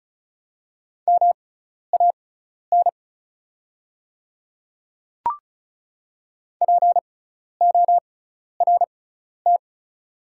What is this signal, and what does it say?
Morse code, heard as a keyed tone, about 34 wpm, spaced as for 12 wpm, 700 Hz. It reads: MAN PORT
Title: 6x Character Spacing - Top 500 Words - 35wpm